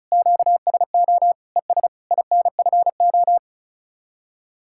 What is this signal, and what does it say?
Morse code, heard as a keyed tone, about 45 wpm, 700 Hz. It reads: QSO ES INFO